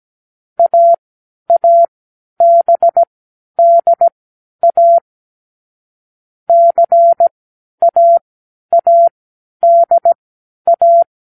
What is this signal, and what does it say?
Morse code, heard as a keyed tone, about 17 wpm, spaced as for 10 wpm, 680 Hz. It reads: AABDA CAADA
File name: k7qo / Lesson9(A-F)